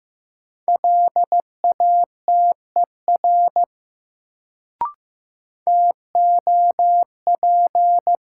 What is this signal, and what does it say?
Morse code, heard as a keyed tone, about 15 wpm, 700 Hz. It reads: LATER TOP
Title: Sentences from Top 700 Words - New Word Drill - 15wpm